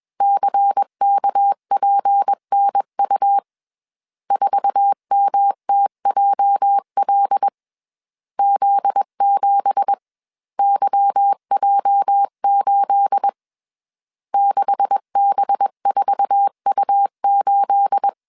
Morse code, heard as a keyed tone, about 21 wpm, 800 Hz. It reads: CKPNU 4MTJL Z7YJ8 6B4U8